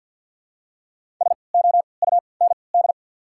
Morse code, heard as a keyed tone, about 48 wpm, spaced as for 28 wpm, 700 Hz. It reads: SOUND
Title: 3x Character Spacing - Top 100 Words - 50wpm